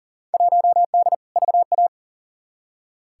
Morse code, heard as a keyed tone, about 40 wpm, 700 Hz. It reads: 1D VA
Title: Field Day Exchanges - 40wpm